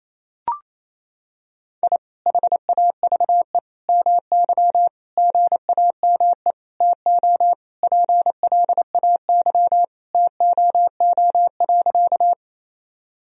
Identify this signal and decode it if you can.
Morse code, heard as a keyed tone, about 28 wpm, 700 Hz. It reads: I HAVE MY GAME TO PLAY TOO.
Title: Sentences from Top 1000 Words - Review Part 3 - 28wpm